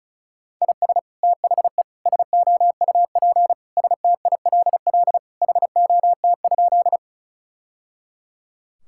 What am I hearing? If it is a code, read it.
Morse code, heard as a keyed tone, about 36 wpm, 700 Hz. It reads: IS THE SOUP STILL HOT?
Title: Sentences from Top 1000 Words - Review Part 6 - 35wpm